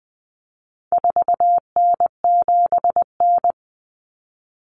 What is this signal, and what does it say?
Morse code, heard as a keyed tone, about 20 wpm, 700 Hz. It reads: 4N7N
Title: Common Cut Numbers - 4-Digit 20wpm